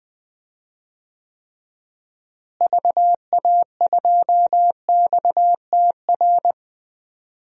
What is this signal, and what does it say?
Morse code, heard as a keyed tone, about 20 wpm, 700 Hz. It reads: VA2XTR
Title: DXCC-100 Call Signs - 6 Characters - Repeated 3x - 20wpm